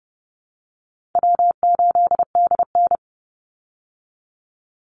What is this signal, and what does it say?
Morse code, heard as a keyed tone, about 30 wpm, 700 Hz. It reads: W8DN
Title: Easy Call Signs Repeated 3x - 30wpm